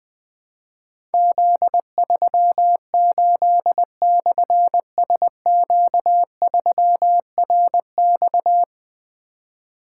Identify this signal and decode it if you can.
Morse code, heard as a keyed tone, about 20 wpm, 700 Hz. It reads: Z38/SQ3RX
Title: Difficult Call Signs 20wpm